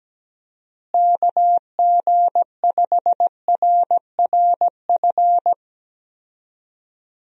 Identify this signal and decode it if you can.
Morse code, heard as a keyed tone, about 17 wpm, 700 Hz. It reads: KG5RRF
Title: USA Call Signs - 6 Characters - Repeated 2x - 17wpm